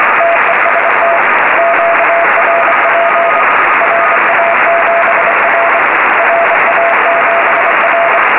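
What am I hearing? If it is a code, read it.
Morse code, heard as a keyed tone, about 25 wpm, 680 Hz, with noise. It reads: TEST QRP RA7R RA7R
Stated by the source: Morse on 20 m in a QRP sprint contest, received on FT-817 at 5 W with a 40 m long-wire, local noise S9+.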